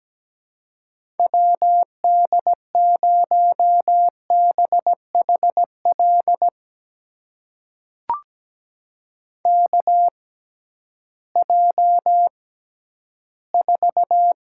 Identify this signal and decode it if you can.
Morse code, heard as a keyed tone, about 17 wpm, spaced as for 11 wpm, 700 Hz. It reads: WD0BHL KJ4
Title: USA Call Signs - 6 Characters - 6x Character Spacing - 17wpm